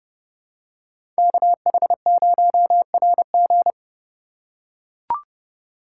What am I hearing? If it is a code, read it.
Morse code, heard as a keyed tone, about 29 wpm, 700 Hz. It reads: KH0RG